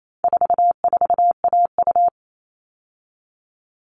Morse code, heard as a keyed tone, about 28 wpm, 700 Hz. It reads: 44AU
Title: All Cut Numbers - 4-Digit 28wpm